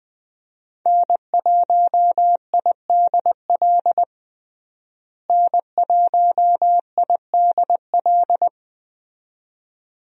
Morse code, heard as a keyed tone, about 20 wpm, 700 Hz. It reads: N1IDL N1IDL
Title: USA Call Signs - 5 Characters - Repeated 2x - 20wpm